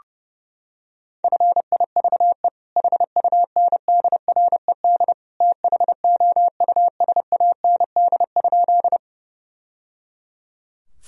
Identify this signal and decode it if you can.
Morse code, heard as a keyed tone, about 30 wpm, 700 Hz. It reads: FIVE HUNDRED THOUSAND?
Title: Sentences from Top 1000 Words - Review Part 3 - 30wpm